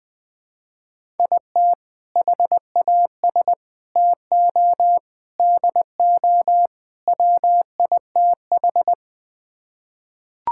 Morse code, heard as a keyed tone, about 21 wpm, 700 Hz. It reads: IT HAS TO DO WITH
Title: Build A Phrase - 5-Word Phrases - 20wpm